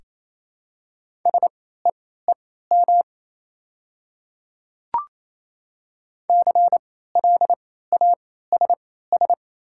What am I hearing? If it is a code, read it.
Morse code, heard as a keyed tone, about 28 wpm, spaced as for 16 wpm, 700 Hz. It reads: SEEM CLASS
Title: 3x Character Spacing - Top 500 Words - 28wpm